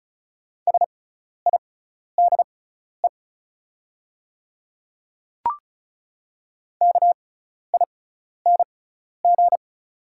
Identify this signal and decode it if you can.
Morse code, heard as a keyed tone, about 36 wpm, spaced as for 12 wpm, 700 Hz. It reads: SIDE KING
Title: 6x Character Spacing - Top 500 Words - 35wpm